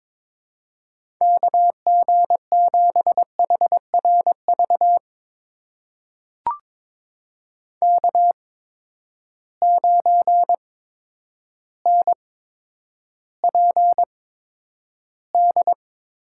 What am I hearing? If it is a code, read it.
Morse code, heard as a keyed tone, about 22 wpm, spaced as for 10 wpm, 700 Hz. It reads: KG7HRV K9NPD
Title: USA Call Signs - 8x Character Spacing - 22wpm